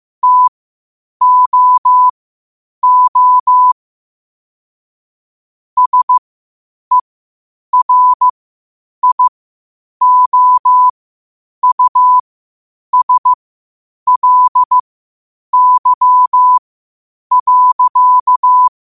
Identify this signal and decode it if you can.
Morse code, heard as a keyed tone, about 15 wpm, spaced as for 8 wpm, 1000 Hz. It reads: TOO SERIOUSLY.
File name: qotdfiles / 8wpm